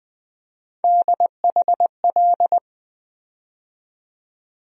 Morse code, heard as a keyed tone, about 20 wpm, 700 Hz. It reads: DHL
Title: USA Call Sign Postfixes - 3 Letters 20wpm